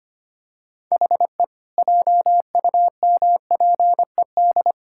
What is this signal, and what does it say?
Morse code, heard as a keyed tone, about 25 wpm, 700 Hz. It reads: HE JUMPED
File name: practice / 25wpm / Sentences with 2-Word Phrases - 25wpm